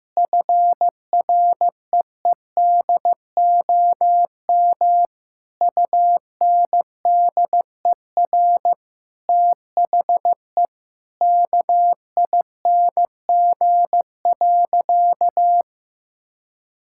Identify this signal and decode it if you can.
Morse code, heard as a keyed tone, about 15 wpm, 700 Hz. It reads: FREEDOM UNDER THE KING.
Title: Sentences from Top 1000 Words - Review Part 6 - 15wpm